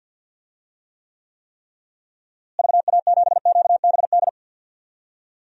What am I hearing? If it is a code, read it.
Morse code, heard as a keyed tone, about 47 wpm, 700 Hz. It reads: VA7QBD
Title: DXCC-100 Call Signs - 6 Characters - Repeated 3x - 50wpm